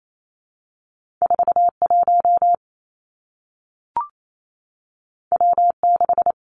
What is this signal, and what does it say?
Morse code, heard as a keyed tone, about 28 wpm, 700 Hz. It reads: 4 1 W 6
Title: Two Challenging Letter-Number Combinations 28wpm